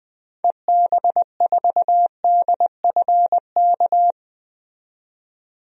Morse code, heard as a keyed tone, about 20 wpm, 700 Hz. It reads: EB4DFK
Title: DXCC-100 Call Signs - 6 Characters - 20wpm